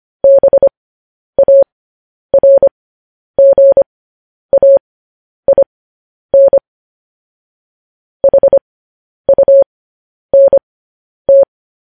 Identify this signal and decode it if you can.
Morse code, heard as a keyed tone, about 25 wpm, spaced as for 10 wpm, 550 Hz. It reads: BARGAIN HUNT
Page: news-headlines-in-morse-code-at-10-wpm